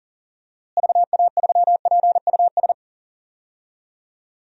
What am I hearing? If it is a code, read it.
Morse code, heard as a keyed tone, about 37 wpm, 700 Hz. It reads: VA3PUS
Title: DXCC-100 Call Signs - 6 Characters - Repeated 2x - 40wpm